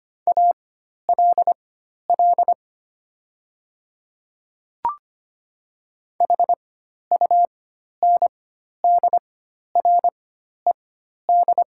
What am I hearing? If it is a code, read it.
Morse code, heard as a keyed tone, about 25 wpm, spaced as for 12 wpm, 700 Hz. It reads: ALL HUNDRED
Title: 4x Character Spacing - Top 500 Words - 25wpm